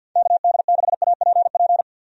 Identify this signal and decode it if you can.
Morse code, heard as a keyed tone, about 52 wpm, 700 Hz. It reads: KD6APP
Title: USA Call Signs - 6 Characters - 50wpm